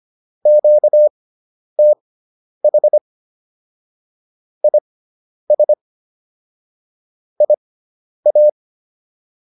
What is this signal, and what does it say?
Morse code, heard as a keyed tone, about 25 wpm, spaced as for 10 wpm, 600 Hz. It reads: QTH IS IA